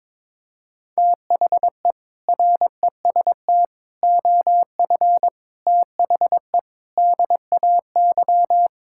Morse code, heard as a keyed tone, about 22 wpm, 700 Hz. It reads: THE REST OF THE DAY
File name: practice / 22wpm / Build A Phrase - 5-Word Phrases - 22wpm